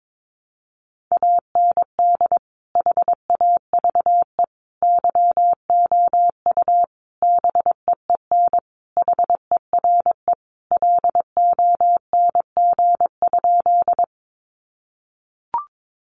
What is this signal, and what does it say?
Morse code, heard as a keyed tone, about 22 wpm, 700 Hz. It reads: AND HAVE YOU BEEN HERE LONG?
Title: Sentences from Top 200 Words 22wpm